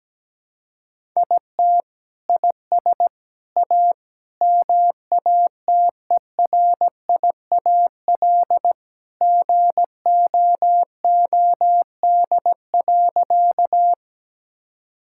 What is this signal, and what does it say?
Morse code, heard as a keyed tone, about 17 wpm, 700 Hz. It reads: IT IS A MATERIAL GOOD.
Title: Sentences from Top 1000 Words - Review Part 5 - 17wpm